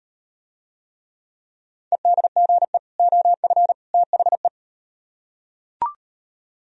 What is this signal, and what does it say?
Morse code, heard as a keyed tone, about 38 wpm, 700 Hz. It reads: EDGE OF THE